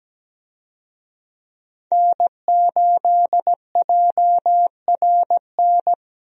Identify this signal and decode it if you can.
Morse code, heard as a keyed tone, about 17 wpm, 700 Hz. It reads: N8JRN
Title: USA Call Signs - 5 Characters - 17wpm